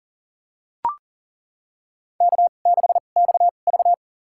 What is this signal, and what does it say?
Morse code, heard as a keyed tone, about 40 wpm, spaced as for 29 wpm, 700 Hz. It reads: K6XV
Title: USA Call Signs - 4 Characters - 2x Character Spacing - 40wpm